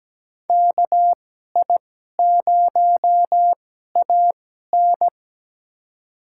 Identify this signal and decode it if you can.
Morse code, heard as a keyed tone, about 17 wpm, spaced as for 12 wpm, 700 Hz. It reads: KI0AN
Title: USA Call Signs - 5 Characters - 2x Character Spacing - 17wpm